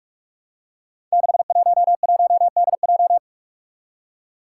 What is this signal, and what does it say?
Morse code, heard as a keyed tone, about 46 wpm, 700 Hz. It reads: 611DJ
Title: Five Challenging Letter-Number Combinations 45wpm